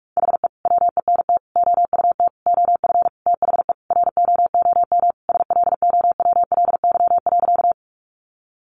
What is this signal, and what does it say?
Morse code, heard as a keyed tone, about 44 wpm, 700 Hz. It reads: HE WENT OUT OF THE ROOM SLOWLY.
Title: Sentences from Top 1100 Words 45wpm